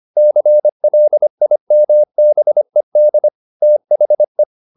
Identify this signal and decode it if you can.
Morse code, heard as a keyed tone, about 25 wpm, 600 Hz. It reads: CLIMBED THE